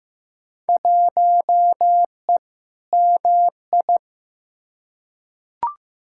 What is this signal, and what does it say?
Morse code, heard as a keyed tone, about 15 wpm, 700 Hz. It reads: 1E MI
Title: Field Day Exchanges - 15wpm